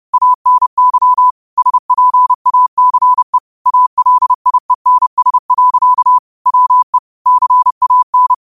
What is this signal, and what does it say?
Morse code, heard as a keyed tone, about 30 wpm, 1000 Hz. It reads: ANY SPACE ALIENS. WE CAN
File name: qotdfiles / 30wpm